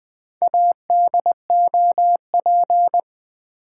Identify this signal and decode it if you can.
Morse code, heard as a keyed tone, about 20 wpm, 700 Hz. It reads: ADOP
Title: Build A Word - 1 Letter At A Time - 5-Letter Words - 20wpm